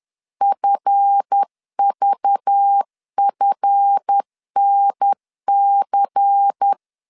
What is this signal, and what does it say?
Morse code, heard as a keyed tone, about 10 wpm, 800 Hz. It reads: FVFNC
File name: LEZIONI / 7 GF9Z / 7 GF 10wpm 2514